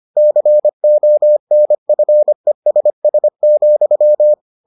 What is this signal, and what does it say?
Morse code, heard as a keyed tone, about 25 wpm, 600 Hz. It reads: CONFESS,